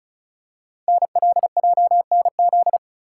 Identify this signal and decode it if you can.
Morse code, heard as a keyed tone, about 36 wpm, 690 Hz. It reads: NLJNZ
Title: Code Groups - 5 Letters - 35wpm